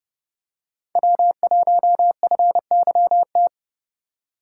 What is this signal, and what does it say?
Morse code, heard as a keyed tone, about 29 wpm, 700 Hz. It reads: W1FYT